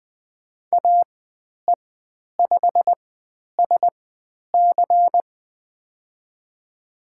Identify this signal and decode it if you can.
Morse code, heard as a keyed tone, about 20 wpm, spaced as for 10 wpm, 700 Hz. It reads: AE5SC
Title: 095 Lesson 16g - Letter c - Call Signs 20wpm